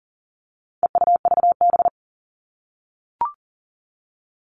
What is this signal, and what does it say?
Morse code, heard as a keyed tone, about 40 wpm, 700 Hz. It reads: EUVB
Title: All Cut Numbers - 4-Digit 40wpm